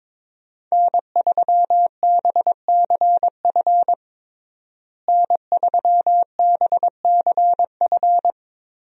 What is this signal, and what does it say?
Morse code, heard as a keyed tone, about 21 wpm, 700 Hz. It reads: N3BCF N3BCF
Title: USA Call Signs - 5 Characters - Repeated 3x - 22wpm